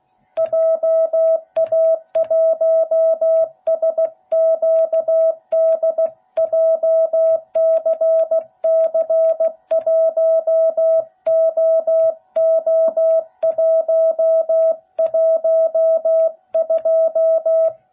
Morse code, heard as a keyed tone, about 16 wpm, 660 Hz. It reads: JA1SQD JCC1OO112